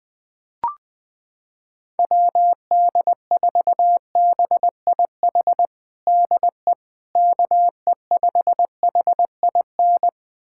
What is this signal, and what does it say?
Morse code, heard as a keyed tone, about 20 wpm, 700 Hz. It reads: WD4BIH DE KE5HIN
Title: USA Callsign DE Callsign 20wpm